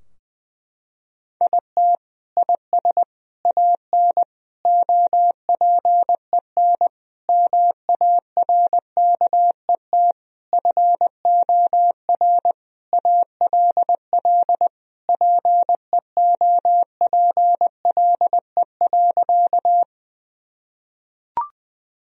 Morse code, heard as a keyed tone, about 21 wpm, 700 Hz. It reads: IT IS AN OPEN MARKET FOR ALL PEOPLE.
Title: Sentences from Top 1000 Words - Review Part 3 - 20wpm